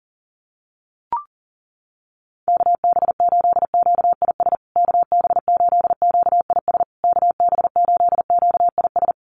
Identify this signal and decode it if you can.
Morse code, heard as a keyed tone, about 40 wpm, 700 Hz. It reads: KB8QIS KB8QIS KB8QIS
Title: USA Call Signs Repeated 3x 40wpm